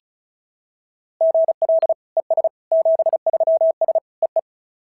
Morse code, heard as a keyed tone, about 33 wpm, 660 Hz. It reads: GL ES 73S EE
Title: SOTA QSO Practice 35wpm